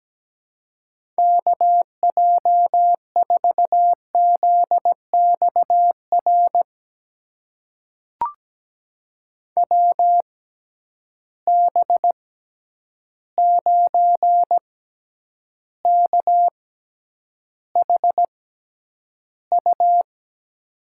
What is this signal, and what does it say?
Morse code, heard as a keyed tone, about 17 wpm, spaced as for 9 wpm, 700 Hz. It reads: KJ4ZXR WB9KHU
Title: USA Call Signs - 6 Characters - 6x Character Spacing - 17wpm